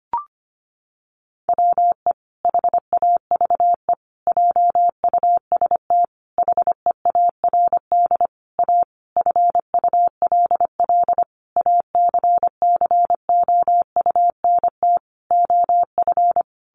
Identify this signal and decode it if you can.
Morse code, heard as a keyed tone, about 25 wpm, 700 Hz. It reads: WE HAVE JUST HEARD A FULL ACCOUNT OF
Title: Sentences from Top 900 Words 25wpm